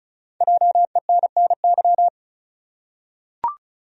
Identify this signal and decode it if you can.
Morse code, heard as a keyed tone, about 35 wpm, 700 Hz. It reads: JENNY
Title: USA Common Names - Max 5 Letters - 35wpm